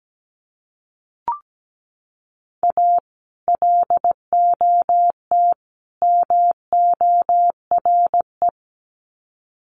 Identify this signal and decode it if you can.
Morse code, heard as a keyed tone, about 17 wpm, 700 Hz. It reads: A LOT MORE
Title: Sets of 3 Words 17wpm